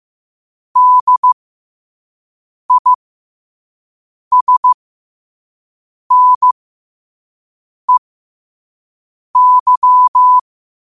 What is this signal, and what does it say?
Morse code, heard as a keyed tone, about 15 wpm, spaced as for 5 wpm, 1000 Hz. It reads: DISNEY
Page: Quotes for Fri, 15 Aug 2025 in Morse Code at 5 words per minute.